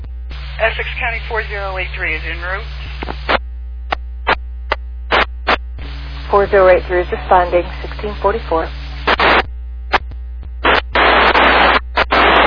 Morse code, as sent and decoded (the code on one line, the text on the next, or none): none